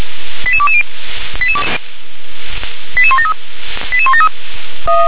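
Five-tone selective call (ZVEI), heard as quick runs of five tones, tones of about 70 ms. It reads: 88220 88173 88173
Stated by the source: ZVEI five-tone selective call, code 88126, received with an SDR.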